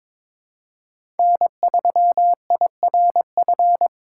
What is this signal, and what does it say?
Morse code, heard as a keyed tone, about 22 wpm, 700 Hz. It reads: N3IRF
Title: USA Call Signs - 5 Characters - 8x Character Spacing - 22wpm